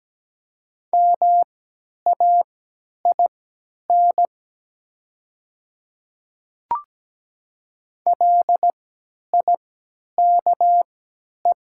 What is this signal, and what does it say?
Morse code, heard as a keyed tone, about 17 wpm, spaced as for 10 wpm, 700 Hz. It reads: MAIN LIKE
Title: 3x Character Spacing - Top 300 Words - 17wpm